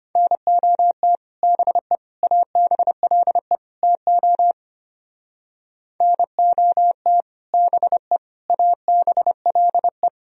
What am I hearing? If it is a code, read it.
Morse code, heard as a keyed tone, about 30 wpm, 700 Hz. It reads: NOT BE ABLE TO NOT BE ABLE
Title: Speed-Racing - Sets of 4 Words 25wpm